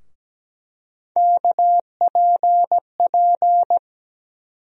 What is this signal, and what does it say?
Morse code, heard as a keyed tone, about 17 wpm, 700 Hz. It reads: KPP
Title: USA Call Sign Postfixes - 3 Letters 17wpm